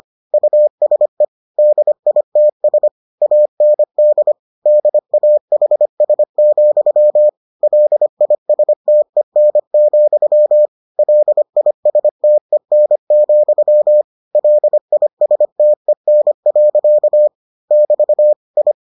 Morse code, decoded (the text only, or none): USE DITS AND DAHS, LISTEN, LISTEN, LISTEN. <BT> I